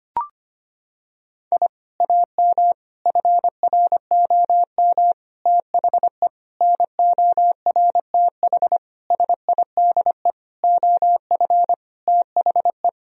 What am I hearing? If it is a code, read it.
Morse code, heard as a keyed tone, about 25 wpm, 700 Hz. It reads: I AM FROM THE NORTH SIDE OF THE
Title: Sentences from Top 1000 Words - Review Part 1 - 25wpm